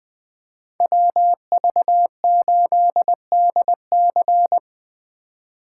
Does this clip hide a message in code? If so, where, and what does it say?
Yes: Morse code from 0.8 to 4.6 s, heard as a keyed tone, about 20 wpm, 700 Hz. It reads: WV8DC